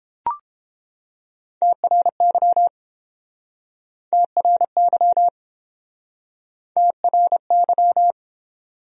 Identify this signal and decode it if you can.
Morse code, heard as a keyed tone, about 30 wpm, 700 Hz. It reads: TRY TRY TRY